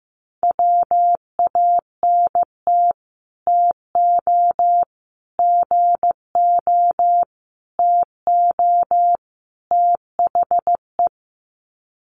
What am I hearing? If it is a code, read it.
Morse code, heard as a keyed tone, about 15 wpm, 700 Hz. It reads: WANT TO GO TO THE
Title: Sets of 5 Words Encore2 15wpm